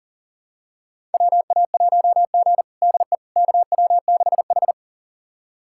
Morse code, heard as a keyed tone, about 40 wpm, 700 Hz. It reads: WA1G DE KW6H
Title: USA Callsign DE Callsign - 4 Characters - 40wpm